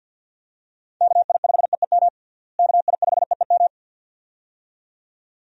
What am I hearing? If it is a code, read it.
Morse code, heard as a keyed tone, about 51 wpm, 700 Hz. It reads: KI5EEM KI5EEM
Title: USA Call Signs - 6 Characters - Repeated 2x - 50wpm